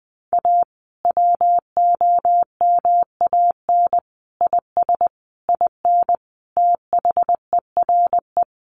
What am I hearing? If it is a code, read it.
Morse code, heard as a keyed tone, about 20 wpm, 700 Hz. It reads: A WOMAN IS IN THERE
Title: Sentences from Top 1400 Words 20wpm